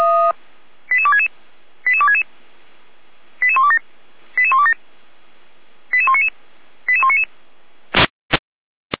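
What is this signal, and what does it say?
Five-tone selective call (ZVEI), heard as quick runs of five tones, tones of about 70 ms. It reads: 88266 88266 88127 88127 88199 88199
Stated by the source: ZVEI five-tone call 88124 received by SDR, archived by feuerwehr-orlamuende.org fire brigade.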